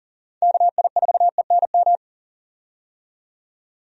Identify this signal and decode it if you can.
Morse code, heard as a keyed tone, about 40 wpm, 700 Hz. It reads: KI4ENM
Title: USA Call Signs - 6 Characters - 40wpm